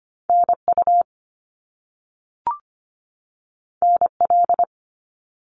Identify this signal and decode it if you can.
Morse code, heard as a keyed tone, about 25 wpm, 700 Hz. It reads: NU NL
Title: Canadian Provinces and Territories 25wpm